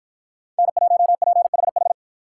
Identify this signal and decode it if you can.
Morse code, heard as a keyed tone, about 53 wpm, 700 Hz. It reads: N1PHH